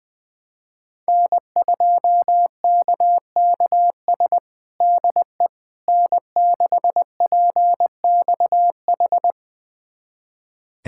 Morse code, heard as a keyed tone, about 20 wpm, 700 Hz. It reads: N2KKS DE N6PXH
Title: USA Callsign DE Callsign - 5 Characters - 20wpm